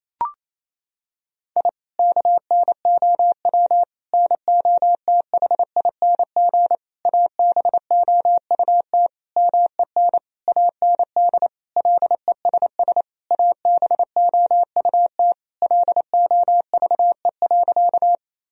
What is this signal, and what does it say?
Morse code, heard as a keyed tone, about 28 wpm, 700 Hz. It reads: I KNOW NOTHING ABOUT MEN AND LESS ABOUT LOVE.
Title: Sentences from Top 1000 Words - Review Part 2 - 28wpm